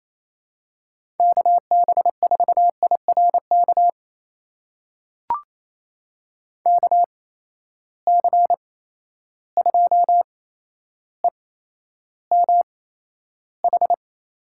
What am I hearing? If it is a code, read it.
Morse code, heard as a keyed tone, about 28 wpm, spaced as for 12 wpm, 700 Hz. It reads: KB4IRK KC2EMH
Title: USA Call Signs - 6 Characters - 8x Character Spacing - 28wpm